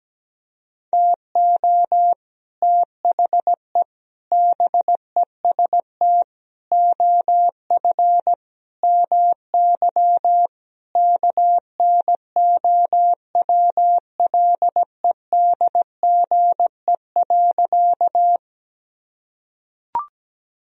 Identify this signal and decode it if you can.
Morse code, heard as a keyed tone, about 17 wpm, 700 Hz. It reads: TO THE BEST OF MY KNOWLEDGE.
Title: Sentences from Top 1000 Words - Review Part 3 - 17wpm